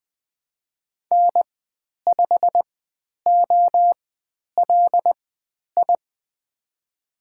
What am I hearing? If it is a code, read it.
Morse code, heard as a keyed tone, about 20 wpm, spaced as for 10 wpm, 700 Hz. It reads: N5OLI
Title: 102 Lesson 17g - Letter u - Call Signs 20wpm